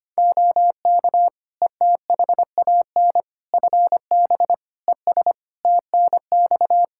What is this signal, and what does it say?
Morse code, heard as a keyed tone, about 25 wpm, 700 Hz. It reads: OK ETHAN FB ES TNX
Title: Practice QSO - Exchange 2 - Info - 25wpm